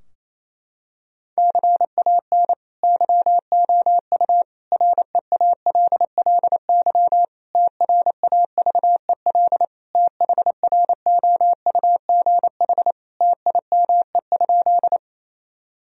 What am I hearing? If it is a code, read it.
Morse code, heard as a keyed tone, about 28 wpm, 700 Hz. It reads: CAN YOU REALLY TRAVEL THROUGH TIME?
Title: Sentences from Top 1000 Words - Review Part 2 - 28wpm